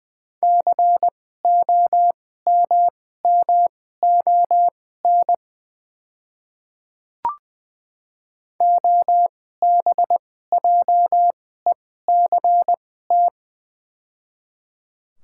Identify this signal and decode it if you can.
Morse code, heard as a keyed tone, about 20 wpm, spaced as for 15 wpm, 700 Hz. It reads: COMMON OBJECT
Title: 2x Character Spacing - Top 500 Words - 20wpm